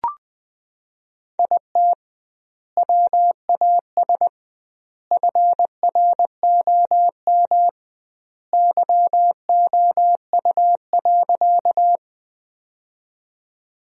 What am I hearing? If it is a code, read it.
Morse code, heard as a keyed tone, about 21 wpm, 700 Hz. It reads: IT WAS FROM YOU.
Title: 2x Word-Spacing - Sentences from Top 100 Words - 20wpm